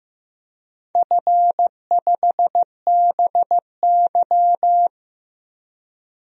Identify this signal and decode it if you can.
Morse code, heard as a keyed tone, about 15 wpm, 700 Hz. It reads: F5BY
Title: DXCC-100 Call Signs - 4 Characters - Repeated 3x - 15wpm